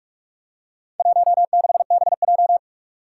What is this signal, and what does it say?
Morse code, heard as a keyed tone, about 42 wpm, 700 Hz. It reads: 16BJ
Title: Four Challenging Letter-Number Combinations 45wpm